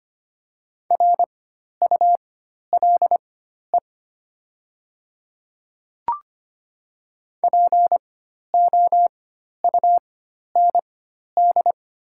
Morse code, heard as a keyed tone, about 25 wpm, spaced as for 12 wpm, 700 Hz. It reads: RULE POUND
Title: 4x Character Spacing - Top 500 Words - 25wpm